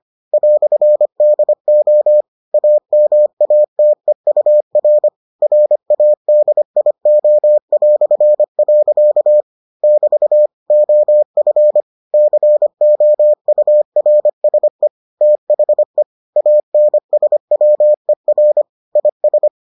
Morse code, heard as a keyed tone, about 25 wpm, 600 Hz. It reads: "DO AMATEUR RADIO". <BT> OF COURSE THE ANSWER IS